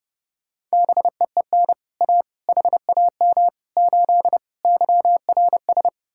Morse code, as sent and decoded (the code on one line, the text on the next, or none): -... . -. / .- / .... .- -- / ---.. / -.-- .-. ...
BEEN A HAM 8 YRS